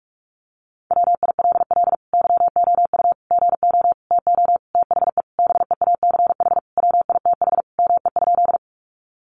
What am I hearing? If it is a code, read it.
Morse code, heard as a keyed tone, about 44 wpm, 700 Hz. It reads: WILL YOU GO TO THE BEACH WITH ME?
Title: Sentences from Top 1500 Words 45wpm